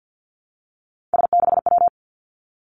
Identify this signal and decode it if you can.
Morse code, heard as a keyed tone, about 50 wpm, 700 Hz. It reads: S6W
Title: Three Challenging Letter-Number Combinations 50wpm